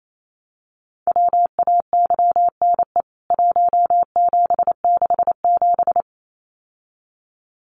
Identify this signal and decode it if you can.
Morse code, heard as a keyed tone, about 28 wpm, 700 Hz. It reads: WAYNE 1767
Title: CWT Exchange 28wpm